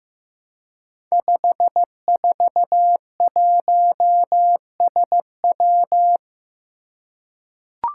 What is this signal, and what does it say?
Morse code, heard as a keyed tone, about 15 wpm, 700 Hz. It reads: 541SW
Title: Five Challenging Letter-Number Combinations 15wpm